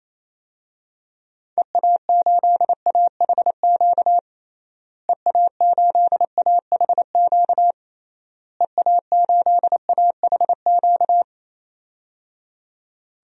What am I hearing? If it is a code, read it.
Morse code, heard as a keyed tone, about 28 wpm, 700 Hz. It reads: EA8AHQ EA8AHQ EA8AHQ
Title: DXCC-100 Call Signs - 6 Characters - Repeated 3x - 28wpm